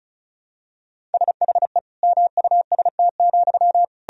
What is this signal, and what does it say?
Morse code, heard as a keyed tone, about 34 wpm, 700 Hz. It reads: SHE MUST,
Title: Sentences from Top 1000 Words - Review Part 4 - 35wpm